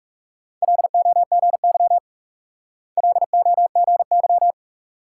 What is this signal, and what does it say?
Morse code, heard as a keyed tone, about 36 wpm, 700 Hz. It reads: LOGY LOGY
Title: Speed-Racing Multi-letter Phonemes - 30wpm